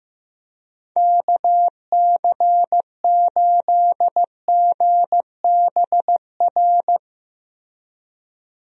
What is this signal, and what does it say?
Morse code, heard as a keyed tone, about 15 wpm, 700 Hz. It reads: KC8GBR
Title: USA Call Signs - 6 Characters - Repeated 3x - 15wpm